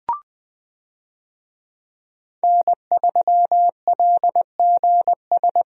Morse code, heard as a keyed tone, about 20 wpm, 700 Hz. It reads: N3LGS
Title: USA Call Signs - 5 Characters - Repeated 3x - 20wpm